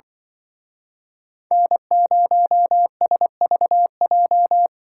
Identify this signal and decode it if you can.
Morse code, heard as a keyed tone, about 24 wpm, 700 Hz. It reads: N0SVJ
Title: Speed Racing - USA 1x3 Call Signs 20wpm